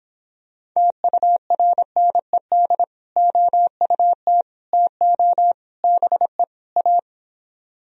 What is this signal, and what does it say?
Morse code, heard as a keyed tone, about 26 wpm, 700 Hz. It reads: TURNED OUT TO BE A